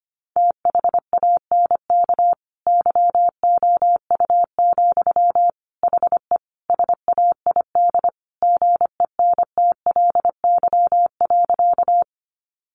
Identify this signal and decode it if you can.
Morse code, heard as a keyed tone, about 25 wpm, 700 Hz. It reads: THANK YOU, HE SAID GENTLY.
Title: Sentences from Top 1500 Words 25wpm